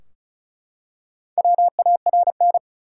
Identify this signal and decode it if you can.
Morse code, heard as a keyed tone, about 34 wpm, 700 Hz. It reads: WARN
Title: Four Letter Words - 35wpm